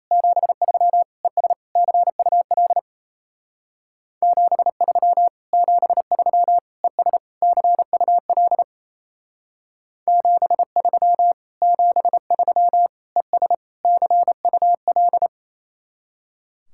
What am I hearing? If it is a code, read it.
Morse code, heard as a keyed tone, about 34 wpm, 700 Hz. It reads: 73 ES CUL 73 73 ES CUL 73 73 ES CUL
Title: Speed-Racing Common QSO Phrases - 28wpm